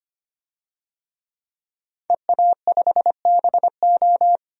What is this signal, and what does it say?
Morse code, heard as a keyed tone, about 25 wpm, 700 Hz. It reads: EA5BO
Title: DXCC-100 Call Signs - 5 Characters - 25wpm